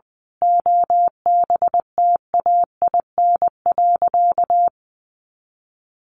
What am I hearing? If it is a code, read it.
Morse code, heard as a keyed tone, about 20 wpm, 700 Hz. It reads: OBTAIN.
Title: Sentences from Top 1300 Words 20wpm